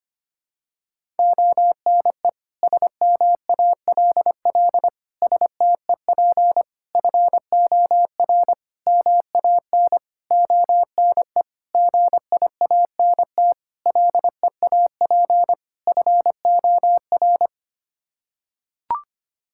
Morse code, heard as a keyed tone, about 25 wpm, 700 Hz. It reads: ONE SMALL STEP FOR MAN ONE GIANT LEAP FOR
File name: Sets of 9 Words - 25wpm